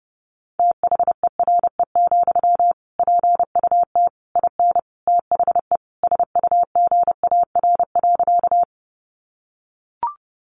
Morse code, heard as a keyed tone, about 30 wpm, 700 Hz. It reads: THERE, PUT IN THE SUGAR.
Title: Sentences from Top 1200 Words 30wpm